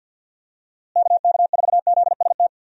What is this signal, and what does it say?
Morse code, heard as a keyed tone, about 51 wpm, 700 Hz. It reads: KK4ZST
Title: USA Call Signs - 6 Characters - Repeated 3x - 50wpm